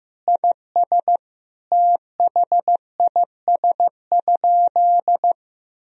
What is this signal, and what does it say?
Morse code, heard as a keyed tone, about 15 wpm, 700 Hz. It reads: IS THIS?
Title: Sentences from Top 1000 Words - Review Part 3 - 15wpm